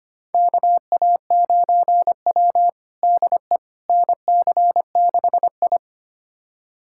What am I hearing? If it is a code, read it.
Morse code, heard as a keyed tone, about 25 wpm, 700 Hz. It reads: KA9W DE NC6I
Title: USA Callsign DE Callsign - 4 Characters - 25wpm